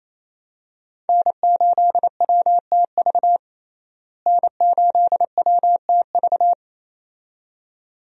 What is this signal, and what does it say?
Morse code, heard as a keyed tone, about 28 wpm, 700 Hz. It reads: N8WTV N8WTV
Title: USA Call Signs - 5 Characters - Repeated 2x - 28wpm